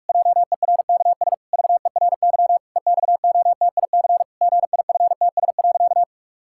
Morse code, heard as a keyed tone, about 46 wpm, 700 Hz. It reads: JERKS VERY EXOTIC GIFTS.
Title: Pangram Challenge v2 - 45wpm